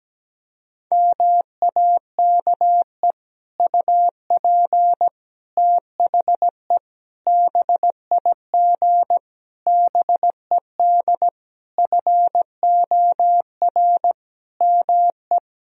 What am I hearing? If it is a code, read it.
Morse code, heard as a keyed tone, about 17 wpm, 700 Hz. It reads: MAKE UP THE BIG BED FOR ME
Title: Sentences from Top 1000 Words - Review Part 1 - 17wpm